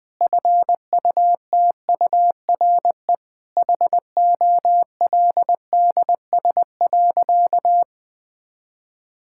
Morse code, heard as a keyed tone, about 20 wpm, 700 Hz. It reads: FUTURE HOLDS.
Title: Sentences with 5-Word Phrases - 20wpm